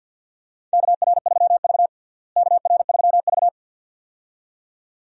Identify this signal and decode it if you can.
Morse code, heard as a keyed tone, about 48 wpm, 700 Hz. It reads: KR3V KR3V
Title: USA Call Signs - 4 Characters - Repeated 2x - 50wpm